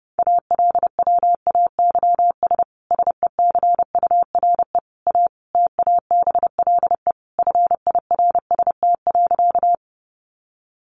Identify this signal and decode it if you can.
Morse code, heard as a keyed tone, about 30 wpm, 700 Hz. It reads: ALWAYS SECURE A TABLE FIRST.
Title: Sentences from Top 1200 Words 30wpm